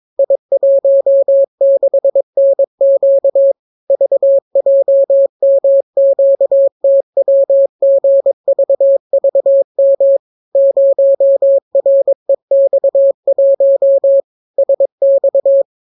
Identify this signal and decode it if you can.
Morse code, heard as a keyed tone, about 22 wpm, 550 Hz. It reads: I16NQ VJMQTWGVVM 0REX1 SX